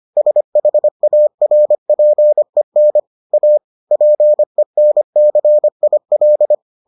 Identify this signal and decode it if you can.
Morse code, heard as a keyed tone, about 25 wpm, 600 Hz. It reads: SHARPEN A PENCIL